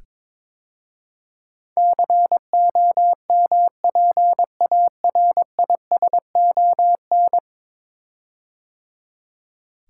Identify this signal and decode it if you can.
Morse code, heard as a keyed tone, about 22 wpm, 700 Hz. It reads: COMPARISON
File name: 10 Letter Words - First 3 Letters Given - 22wpm